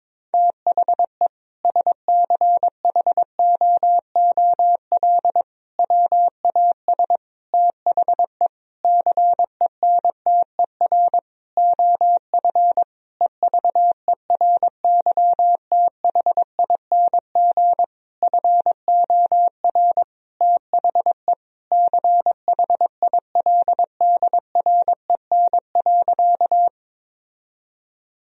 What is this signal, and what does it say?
Morse code, heard as a keyed tone, about 21 wpm, 700 Hz. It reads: THE SCHOOL WAS THE CENTER OF EVERYTHING FOR THE CHILDREN.
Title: Sentences from Top 1000 Words - Review Part 2 - 22wpm